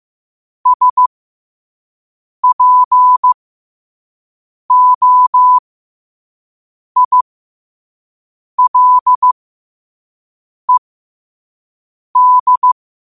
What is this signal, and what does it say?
Morse code, heard as a keyed tone, about 15 wpm, spaced as for 5 wpm, 1000 Hz. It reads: SPOILED
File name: qotdfiles / 5wpm